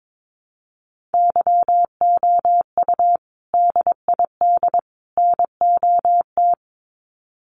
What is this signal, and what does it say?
Morse code, heard as a keyed tone, about 22 wpm, 700 Hz. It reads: YOU DID NOT
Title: Sets of 3 Words 22wpm